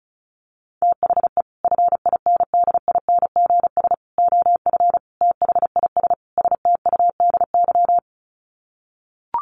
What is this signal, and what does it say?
Morse code, heard as a keyed tone, about 34 wpm, 700 Hz. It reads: THE FINDINGS OF THIS STUDY
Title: Sets of 5 Words Encore2 35wpm